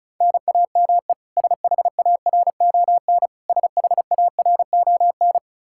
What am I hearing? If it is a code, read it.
Morse code, heard as a keyed tone, about 36 wpm, 700 Hz. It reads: NAME SHARON SHARON
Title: K1USN SST QSO Practice 35wpm